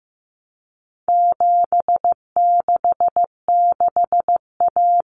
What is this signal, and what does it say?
Morse code, heard as a keyed tone, about 15 wpm, 700 Hz. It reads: 766A
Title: Common Cut Numbers - 4-Digit 15wpm